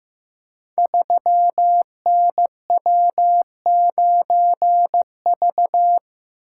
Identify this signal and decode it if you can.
Morse code, heard as a keyed tone, about 15 wpm, 700 Hz. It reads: 3NW9V
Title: Code Groups - 5 Letter:Numbers - 15wpm